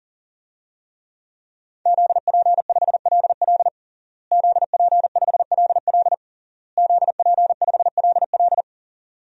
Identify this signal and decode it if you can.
Morse code, heard as a keyed tone, about 40 wpm, 700 Hz. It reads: ZP5LL ZP5LL ZP5LL
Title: DXCC-100 Call Signs - 5 Characters - Repeated 3x - 40wpm